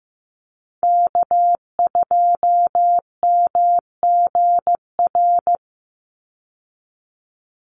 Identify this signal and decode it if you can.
Morse code, heard as a keyed tone, about 15 wpm, 700 Hz. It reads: K2MGR
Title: Call Signs Repeated 3x - 15wpm